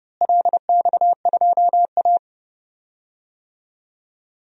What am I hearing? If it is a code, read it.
Morse code, heard as a keyed tone, about 30 wpm, 700 Hz. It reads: LX2A
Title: DXCC-100 Call Signs - 4 Characters - Repeated 2x - 30wpm